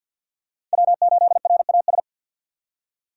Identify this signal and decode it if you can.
Morse code, heard as a keyed tone, about 53 wpm, 700 Hz. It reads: W8RAS
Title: USA Call Signs - 5 Characters - 4x Character Spacing - 50wpm